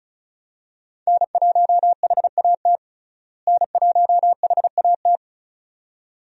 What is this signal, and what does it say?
Morse code, heard as a keyed tone, about 34 wpm, 700 Hz. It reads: N1HAT N1HAT
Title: USA Call Signs - 5 Characters - Repeated 2x - 35wpm